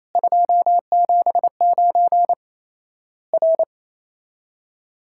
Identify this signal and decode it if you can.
Morse code, heard as a keyed tone, about 28 wpm, 700 Hz. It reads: 279 R
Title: Worked All Europe DX Contest - QTC Exchange - 28wpm